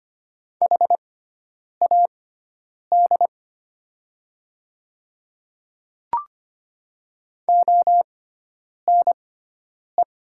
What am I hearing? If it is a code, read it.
Morse code, heard as a keyed tone, about 25 wpm, spaced as for 9 wpm, 700 Hz. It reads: HAD ONE
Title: 6x Character Spacing - Top 100 Words - 25wpm